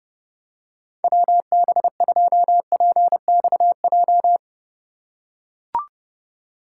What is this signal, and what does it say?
Morse code, heard as a keyed tone, about 28 wpm, 700 Hz. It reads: WB2PXJ